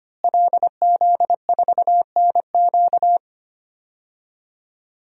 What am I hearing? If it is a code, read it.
Morse code, heard as a keyed tone, about 25 wpm, 700 Hz. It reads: LZ4NQ
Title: DXCC-100 Call Signs - 5 Characters - Repeated 3x - 25wpm